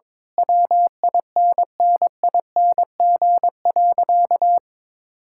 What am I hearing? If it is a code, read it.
Morse code, heard as a keyed tone, about 22 wpm, 700 Hz. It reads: WINNING.
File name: practice / 22wpm / Sentences with 8-Word Phrases - 22wpm